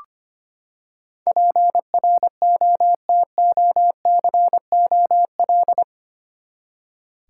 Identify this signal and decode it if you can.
Morse code, heard as a keyed tone, about 25 wpm, 700 Hz. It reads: PROTOCOL
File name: Eight Letter Words - 25wpm